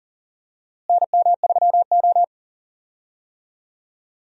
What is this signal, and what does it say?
Morse code, heard as a keyed tone, about 40 wpm, 700 Hz. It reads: NM3O
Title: USA Call Signs - 4 Characters - Repeated 3x - 40wpm